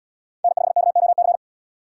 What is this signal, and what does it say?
Morse code, heard as a keyed tone, about 77 wpm, 700 Hz. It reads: N5KOC